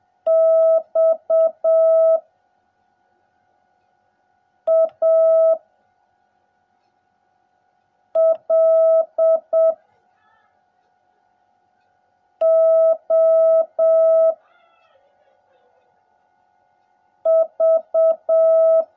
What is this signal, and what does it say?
Morse code, heard as a keyed tone, about 7 wpm, 650 Hz. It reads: X A L O V
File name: jh8cbh / 2021cw / cw16a